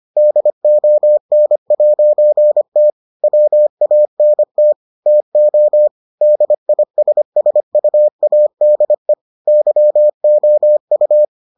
Morse code, heard as a keyed tone, about 25 wpm, 600 Hz. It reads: DON'T WANT TO DISSUADE YOU